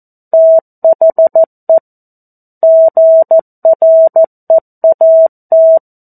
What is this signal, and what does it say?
Morse code, heard as a keyed tone, about 14 wpm, 650 Hz. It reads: THE GREAT
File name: 02-Chapter-02 14 WPM